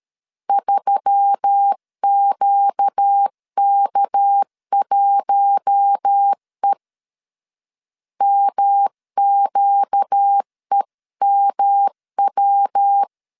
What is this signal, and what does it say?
Morse code, heard as a keyed tone, about 13 wpm, 800 Hz. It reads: 3QK1E MQEMW
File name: CW / DET / 1 6 1500C S C 12wpm 3533